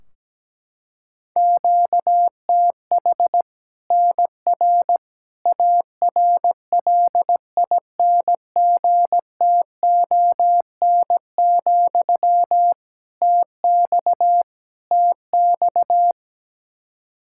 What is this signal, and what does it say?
Morse code, heard as a keyed tone, about 17 wpm, 700 Hz. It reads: QTH NR ARLINGTON, TX TX
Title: K1USN SST QSO Practice 17wpm